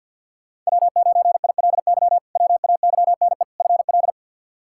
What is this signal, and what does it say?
Morse code, heard as a keyed tone, about 51 wpm, 700 Hz. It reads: W9ILY WAYNE FL
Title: K1USN SST QSO Practice - 50wpm